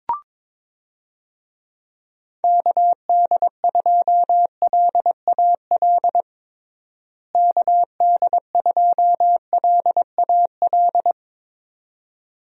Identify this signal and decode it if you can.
Morse code, heard as a keyed tone, about 22 wpm, 700 Hz. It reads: KD2LAL KD2LAL
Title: USA Call Signs - 6 Characters - Repeated 2x - 22wpm